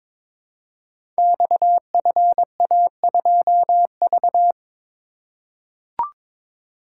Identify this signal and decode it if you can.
Morse code, heard as a keyed tone, about 22 wpm, 690 Hz. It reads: XFA2V
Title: Code Groups - 5 Letter:Numbers - 22wpm